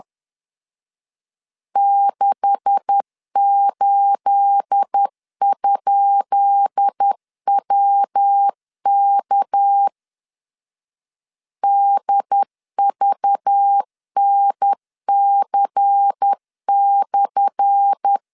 Morse code, heard as a keyed tone, about 10 wpm, 800 Hz. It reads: 68?WK DVNC/
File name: LEZIONI / 10 SIGNS / 10B INT 10wpm 2844